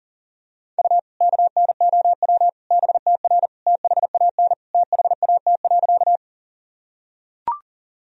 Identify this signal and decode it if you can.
Morse code, heard as a keyed tone, about 40 wpm, 700 Hz. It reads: U KNOW BTR THAN THAT.
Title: Sentences from Top 400 Words and Abbreviations 40wpm